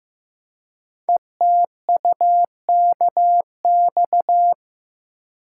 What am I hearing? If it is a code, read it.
Morse code, heard as a keyed tone, about 15 wpm, 700 Hz. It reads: ETUKX
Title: Code Groups - 5 Letters - 15wpm